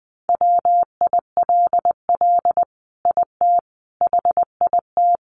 Morse code, heard as a keyed tone, about 20 wpm, 700 Hz. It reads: WILL IT HIT
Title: Sentences from Top 1200 Words 20wpm